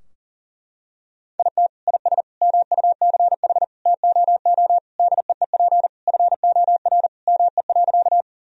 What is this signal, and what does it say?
Morse code, heard as a keyed tone, about 40 wpm, 700 Hz. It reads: IT IS MUCH TOO DEEP FOR ME.
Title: Sentences from Top 1000 Words - Review Part 3 - 40wpm